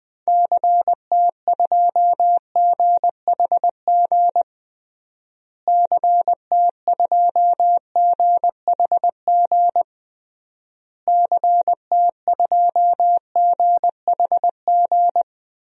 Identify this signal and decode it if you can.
Morse code, heard as a keyed tone, about 20 wpm, 700 Hz. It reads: CT2GHG CT2GHG CT2GHG
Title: DXCC-100 Call Signs - 6 Characters - Repeated 3x - 20wpm